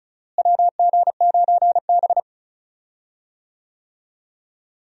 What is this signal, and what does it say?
Morse code, heard as a keyed tone, about 36 wpm, 700 Hz. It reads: WG9B
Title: USA Call Signs - 4 Characters - Repeated 2x - 35wpm